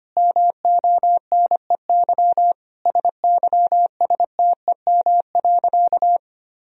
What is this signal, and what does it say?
Morse code, heard as a keyed tone, about 25 wpm, 700 Hz. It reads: MONEY SYSTEM.
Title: Sentences from Top 1000 Words - Review Part 4 - 25wpm